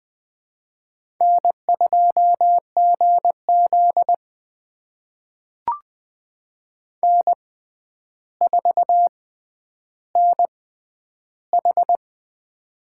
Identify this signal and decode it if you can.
Morse code, heard as a keyed tone, about 20 wpm, spaced as for 10 wpm, 700 Hz. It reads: N2GZ N4NH
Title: USA Call Signs - 4 Characters - 6x Character Spacing - 20wpm